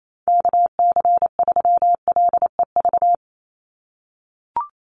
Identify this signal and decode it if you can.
Morse code, heard as a keyed tone, about 28 wpm, 700 Hz. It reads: KC3LEV
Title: USA Call Signs 28wpm